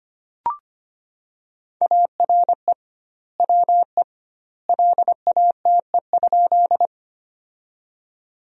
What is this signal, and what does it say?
Morse code, heard as a keyed tone, about 25 wpm, 700 Hz. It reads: ARE WE LATE?
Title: 2x Word-Spacing - Sentences from Top 300 Words - 25wpm